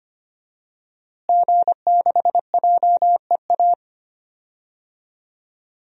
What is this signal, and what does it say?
Morse code, heard as a keyed tone, about 25 wpm, 700 Hz. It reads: G6JEA
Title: DXCC-100 Call Signs - 5 Characters - Repeated 3x - 25wpm